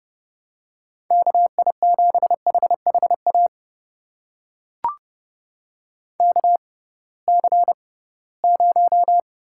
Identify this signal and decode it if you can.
Morse code, heard as a keyed tone, about 31 wpm, spaced as for 20 wpm, 700 Hz. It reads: KI7HHA KC0
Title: USA Call Signs - 6 Characters - 6x Character Spacing - 30wpm